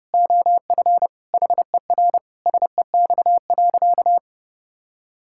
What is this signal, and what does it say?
Morse code, heard as a keyed tone, about 32 wpm, 700 Hz. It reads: OF HER SEX.
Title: Sentences from Top 1600 Words 30wpm